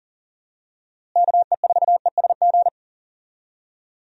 Morse code, heard as a keyed tone, about 37 wpm, 700 Hz. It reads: KE4ESG
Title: USA Call Signs - 6 Characters - Repeated 3x - 40wpm